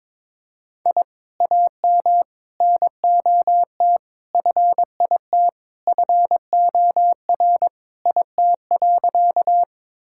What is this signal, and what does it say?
Morse code, heard as a keyed tone, about 22 wpm, 700 Hz. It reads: I AM NOT FIT FOR IT.
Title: Sentences from Top 1000 Words - Review Part 5 - 22wpm